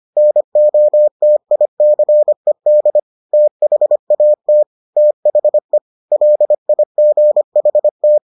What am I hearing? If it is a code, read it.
Morse code, heard as a keyed tone, about 25 wpm, 600 Hz. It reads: NOTICED THAT THE LIGHT